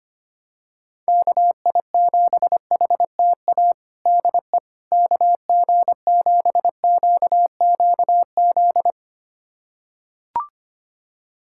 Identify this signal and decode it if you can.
Morse code, heard as a keyed tone, about 25 wpm, 700 Hz. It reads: KI7HTA DE KG7QQZ
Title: USA Callsign DE Callsign - 6 Characters - 25wpm